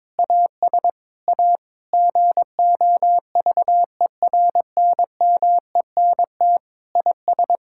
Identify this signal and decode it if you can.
Morse code, heard as a keyed tone, about 22 wpm, 700 Hz. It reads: AS A GOVERNMENT IS
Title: Sentences from Top 1600 Words 22wpm